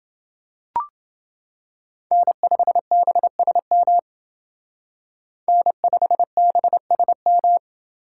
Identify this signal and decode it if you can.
Morse code, heard as a keyed tone, about 28 wpm, 700 Hz. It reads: N5BSM N5BSM